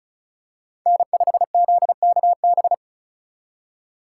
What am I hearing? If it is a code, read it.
Morse code, heard as a keyed tone, about 36 wpm, 700 Hz. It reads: N5ZKB
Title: USA Call Signs - 5 Characters - 3x Character Spacing - 35wpm